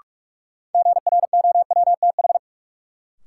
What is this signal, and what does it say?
Morse code, heard as a keyed tone, about 43 wpm, 700 Hz. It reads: GROWTH
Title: Sentences from Top 1300 Words - New Word Drill Alt - 45wpm